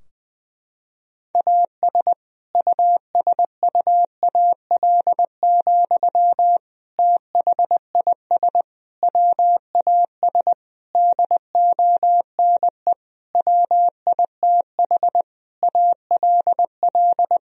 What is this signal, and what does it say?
Morse code, heard as a keyed tone, about 21 wpm, 700 Hz. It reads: AS USUAL, THIS WAS DONE WITH ALL
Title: Sentences from Top 1000 Words - Review Part 2 - 20wpm